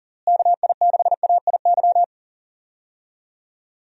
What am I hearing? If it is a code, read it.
Morse code, heard as a keyed tone, about 40 wpm, 700 Hz. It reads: KI6AIY
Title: USA Call Signs - 6 Characters - Repeated 3x - 40wpm